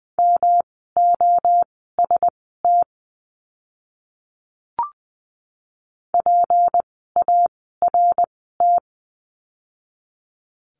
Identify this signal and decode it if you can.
Morse code, heard as a keyed tone, about 20 wpm, spaced as for 14 wpm, 700 Hz. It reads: MOST PART
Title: 237 Lesson 38f - Letter q - Words 20wpm